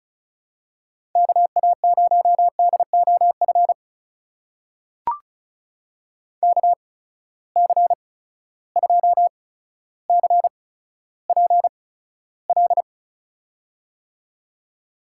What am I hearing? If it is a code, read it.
Morse code, heard as a keyed tone, about 36 wpm, spaced as for 15 wpm, 700 Hz. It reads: KA0DOF KC2CPL
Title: USA Call Signs - 6 Characters - 8x Character Spacing - 35wpm